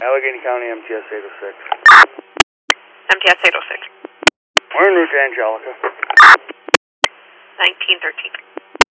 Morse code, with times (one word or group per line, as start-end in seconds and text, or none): none